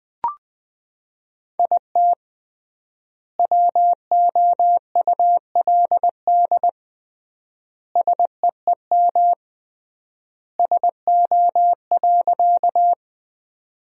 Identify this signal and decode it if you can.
Morse code, heard as a keyed tone, about 20 wpm, 700 Hz. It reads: IT WOULD SEEM SO.
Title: 3x Word-Spacing - Sentences from Top 300 Words - 20wpm